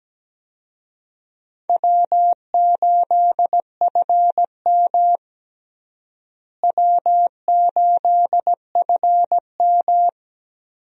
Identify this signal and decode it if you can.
Morse code, heard as a keyed tone, about 17 wpm, 700 Hz. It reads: W8FM W8FM
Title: USA Call Signs - 4 Characters - Repeated 2x - 17wpm